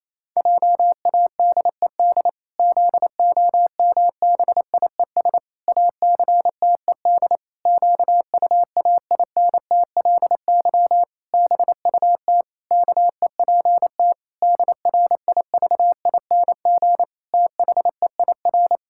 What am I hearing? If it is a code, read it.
Morse code, heard as a keyed tone, about 28 wpm, 700 Hz. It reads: JADED ZOMBIES ACTED QUAINTLY BUT KEPT DRIVING THEIR
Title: Pangram Challenge v2 - 28wpm